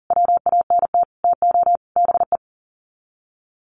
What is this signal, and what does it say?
Morse code, heard as a keyed tone, about 40 wpm, 700 Hz. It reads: WANT TO BE
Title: Sets of 4 Words 40wpm